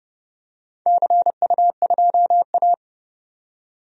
Morse code, heard as a keyed tone, about 30 wpm, 700 Hz. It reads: CU2A
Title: DXCC-100 Call Signs - 4 Characters - 30wpm